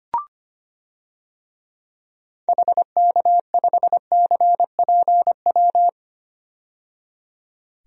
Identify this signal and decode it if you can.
Morse code, heard as a keyed tone, about 25 wpm, 700 Hz. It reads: HK5CPW
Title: DXCC-100 Call Signs - 6 Characters - 25wpm